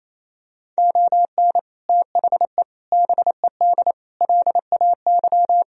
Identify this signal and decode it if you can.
Morse code, heard as a keyed tone, about 28 wpm, 700 Hz. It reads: ON THE BED LAY
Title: Sentences from Top 1000 Words - Review Part 2 - 28wpm